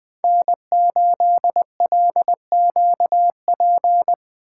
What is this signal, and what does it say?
Morse code, heard as a keyed tone, about 21 wpm, 700 Hz. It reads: N8LQP
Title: USA Call Signs - 5 Characters - 2x Character Spacing - 20wpm